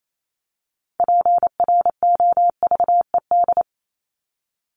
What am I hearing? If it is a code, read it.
Morse code, heard as a keyed tone, about 28 wpm, 700 Hz. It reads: PROVED
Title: ED Suffix - Top 100 Words with Base Words 28wpm